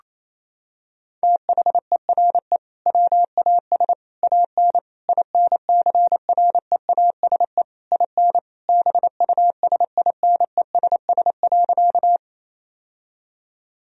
Morse code, heard as a keyed tone, about 28 wpm, 700 Hz. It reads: THERE WAS AN INCREASE IN BUSINESS.
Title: Sentences from Top 1000 Words - Review Part 4 - 28wpm